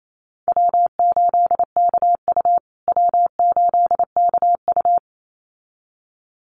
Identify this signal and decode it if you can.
Morse code, heard as a keyed tone, about 27 wpm, 700 Hz. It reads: W8KU W8KU
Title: USA Call Signs Repeated 2x 28wpm